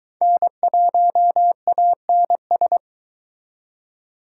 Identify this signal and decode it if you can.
Morse code, heard as a keyed tone, about 23 wpm, 700 Hz. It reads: N1ANS